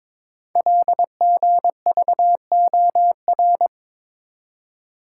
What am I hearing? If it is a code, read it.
Morse code, heard as a keyed tone, about 23 wpm, 700 Hz. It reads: LGVOR
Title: Code Groups - 5 Letters - 22wpm